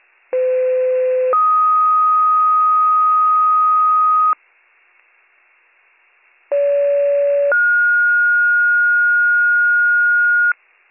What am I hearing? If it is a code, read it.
Two-tone page, heard as a one-second tone, then a three-second tone, about 510 then 1250 Hz, about 570 then 1430 Hz. It is heard from 0.3 to 10.5 s.